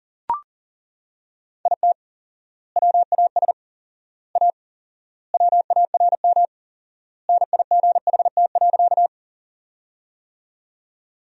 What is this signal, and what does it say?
Morse code, heard as a keyed tone, about 40 wpm, 700 Hz. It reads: IT WAS A WARM NIGHT.
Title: 4x Word-Spacing - Sentences from Top 500 Words - 40wpm